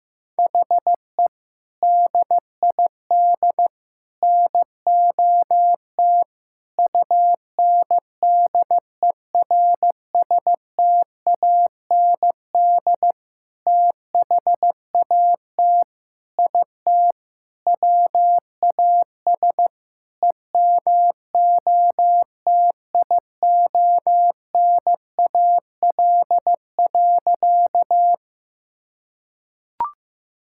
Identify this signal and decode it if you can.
Morse code, heard as a keyed tone, about 15 wpm, 700 Hz. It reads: HE DID NOT UNDERSTAND THAT IT WAS EMOTIONAL.
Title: Sentences from Top 1000 Words - Review Part 5 - 15wpm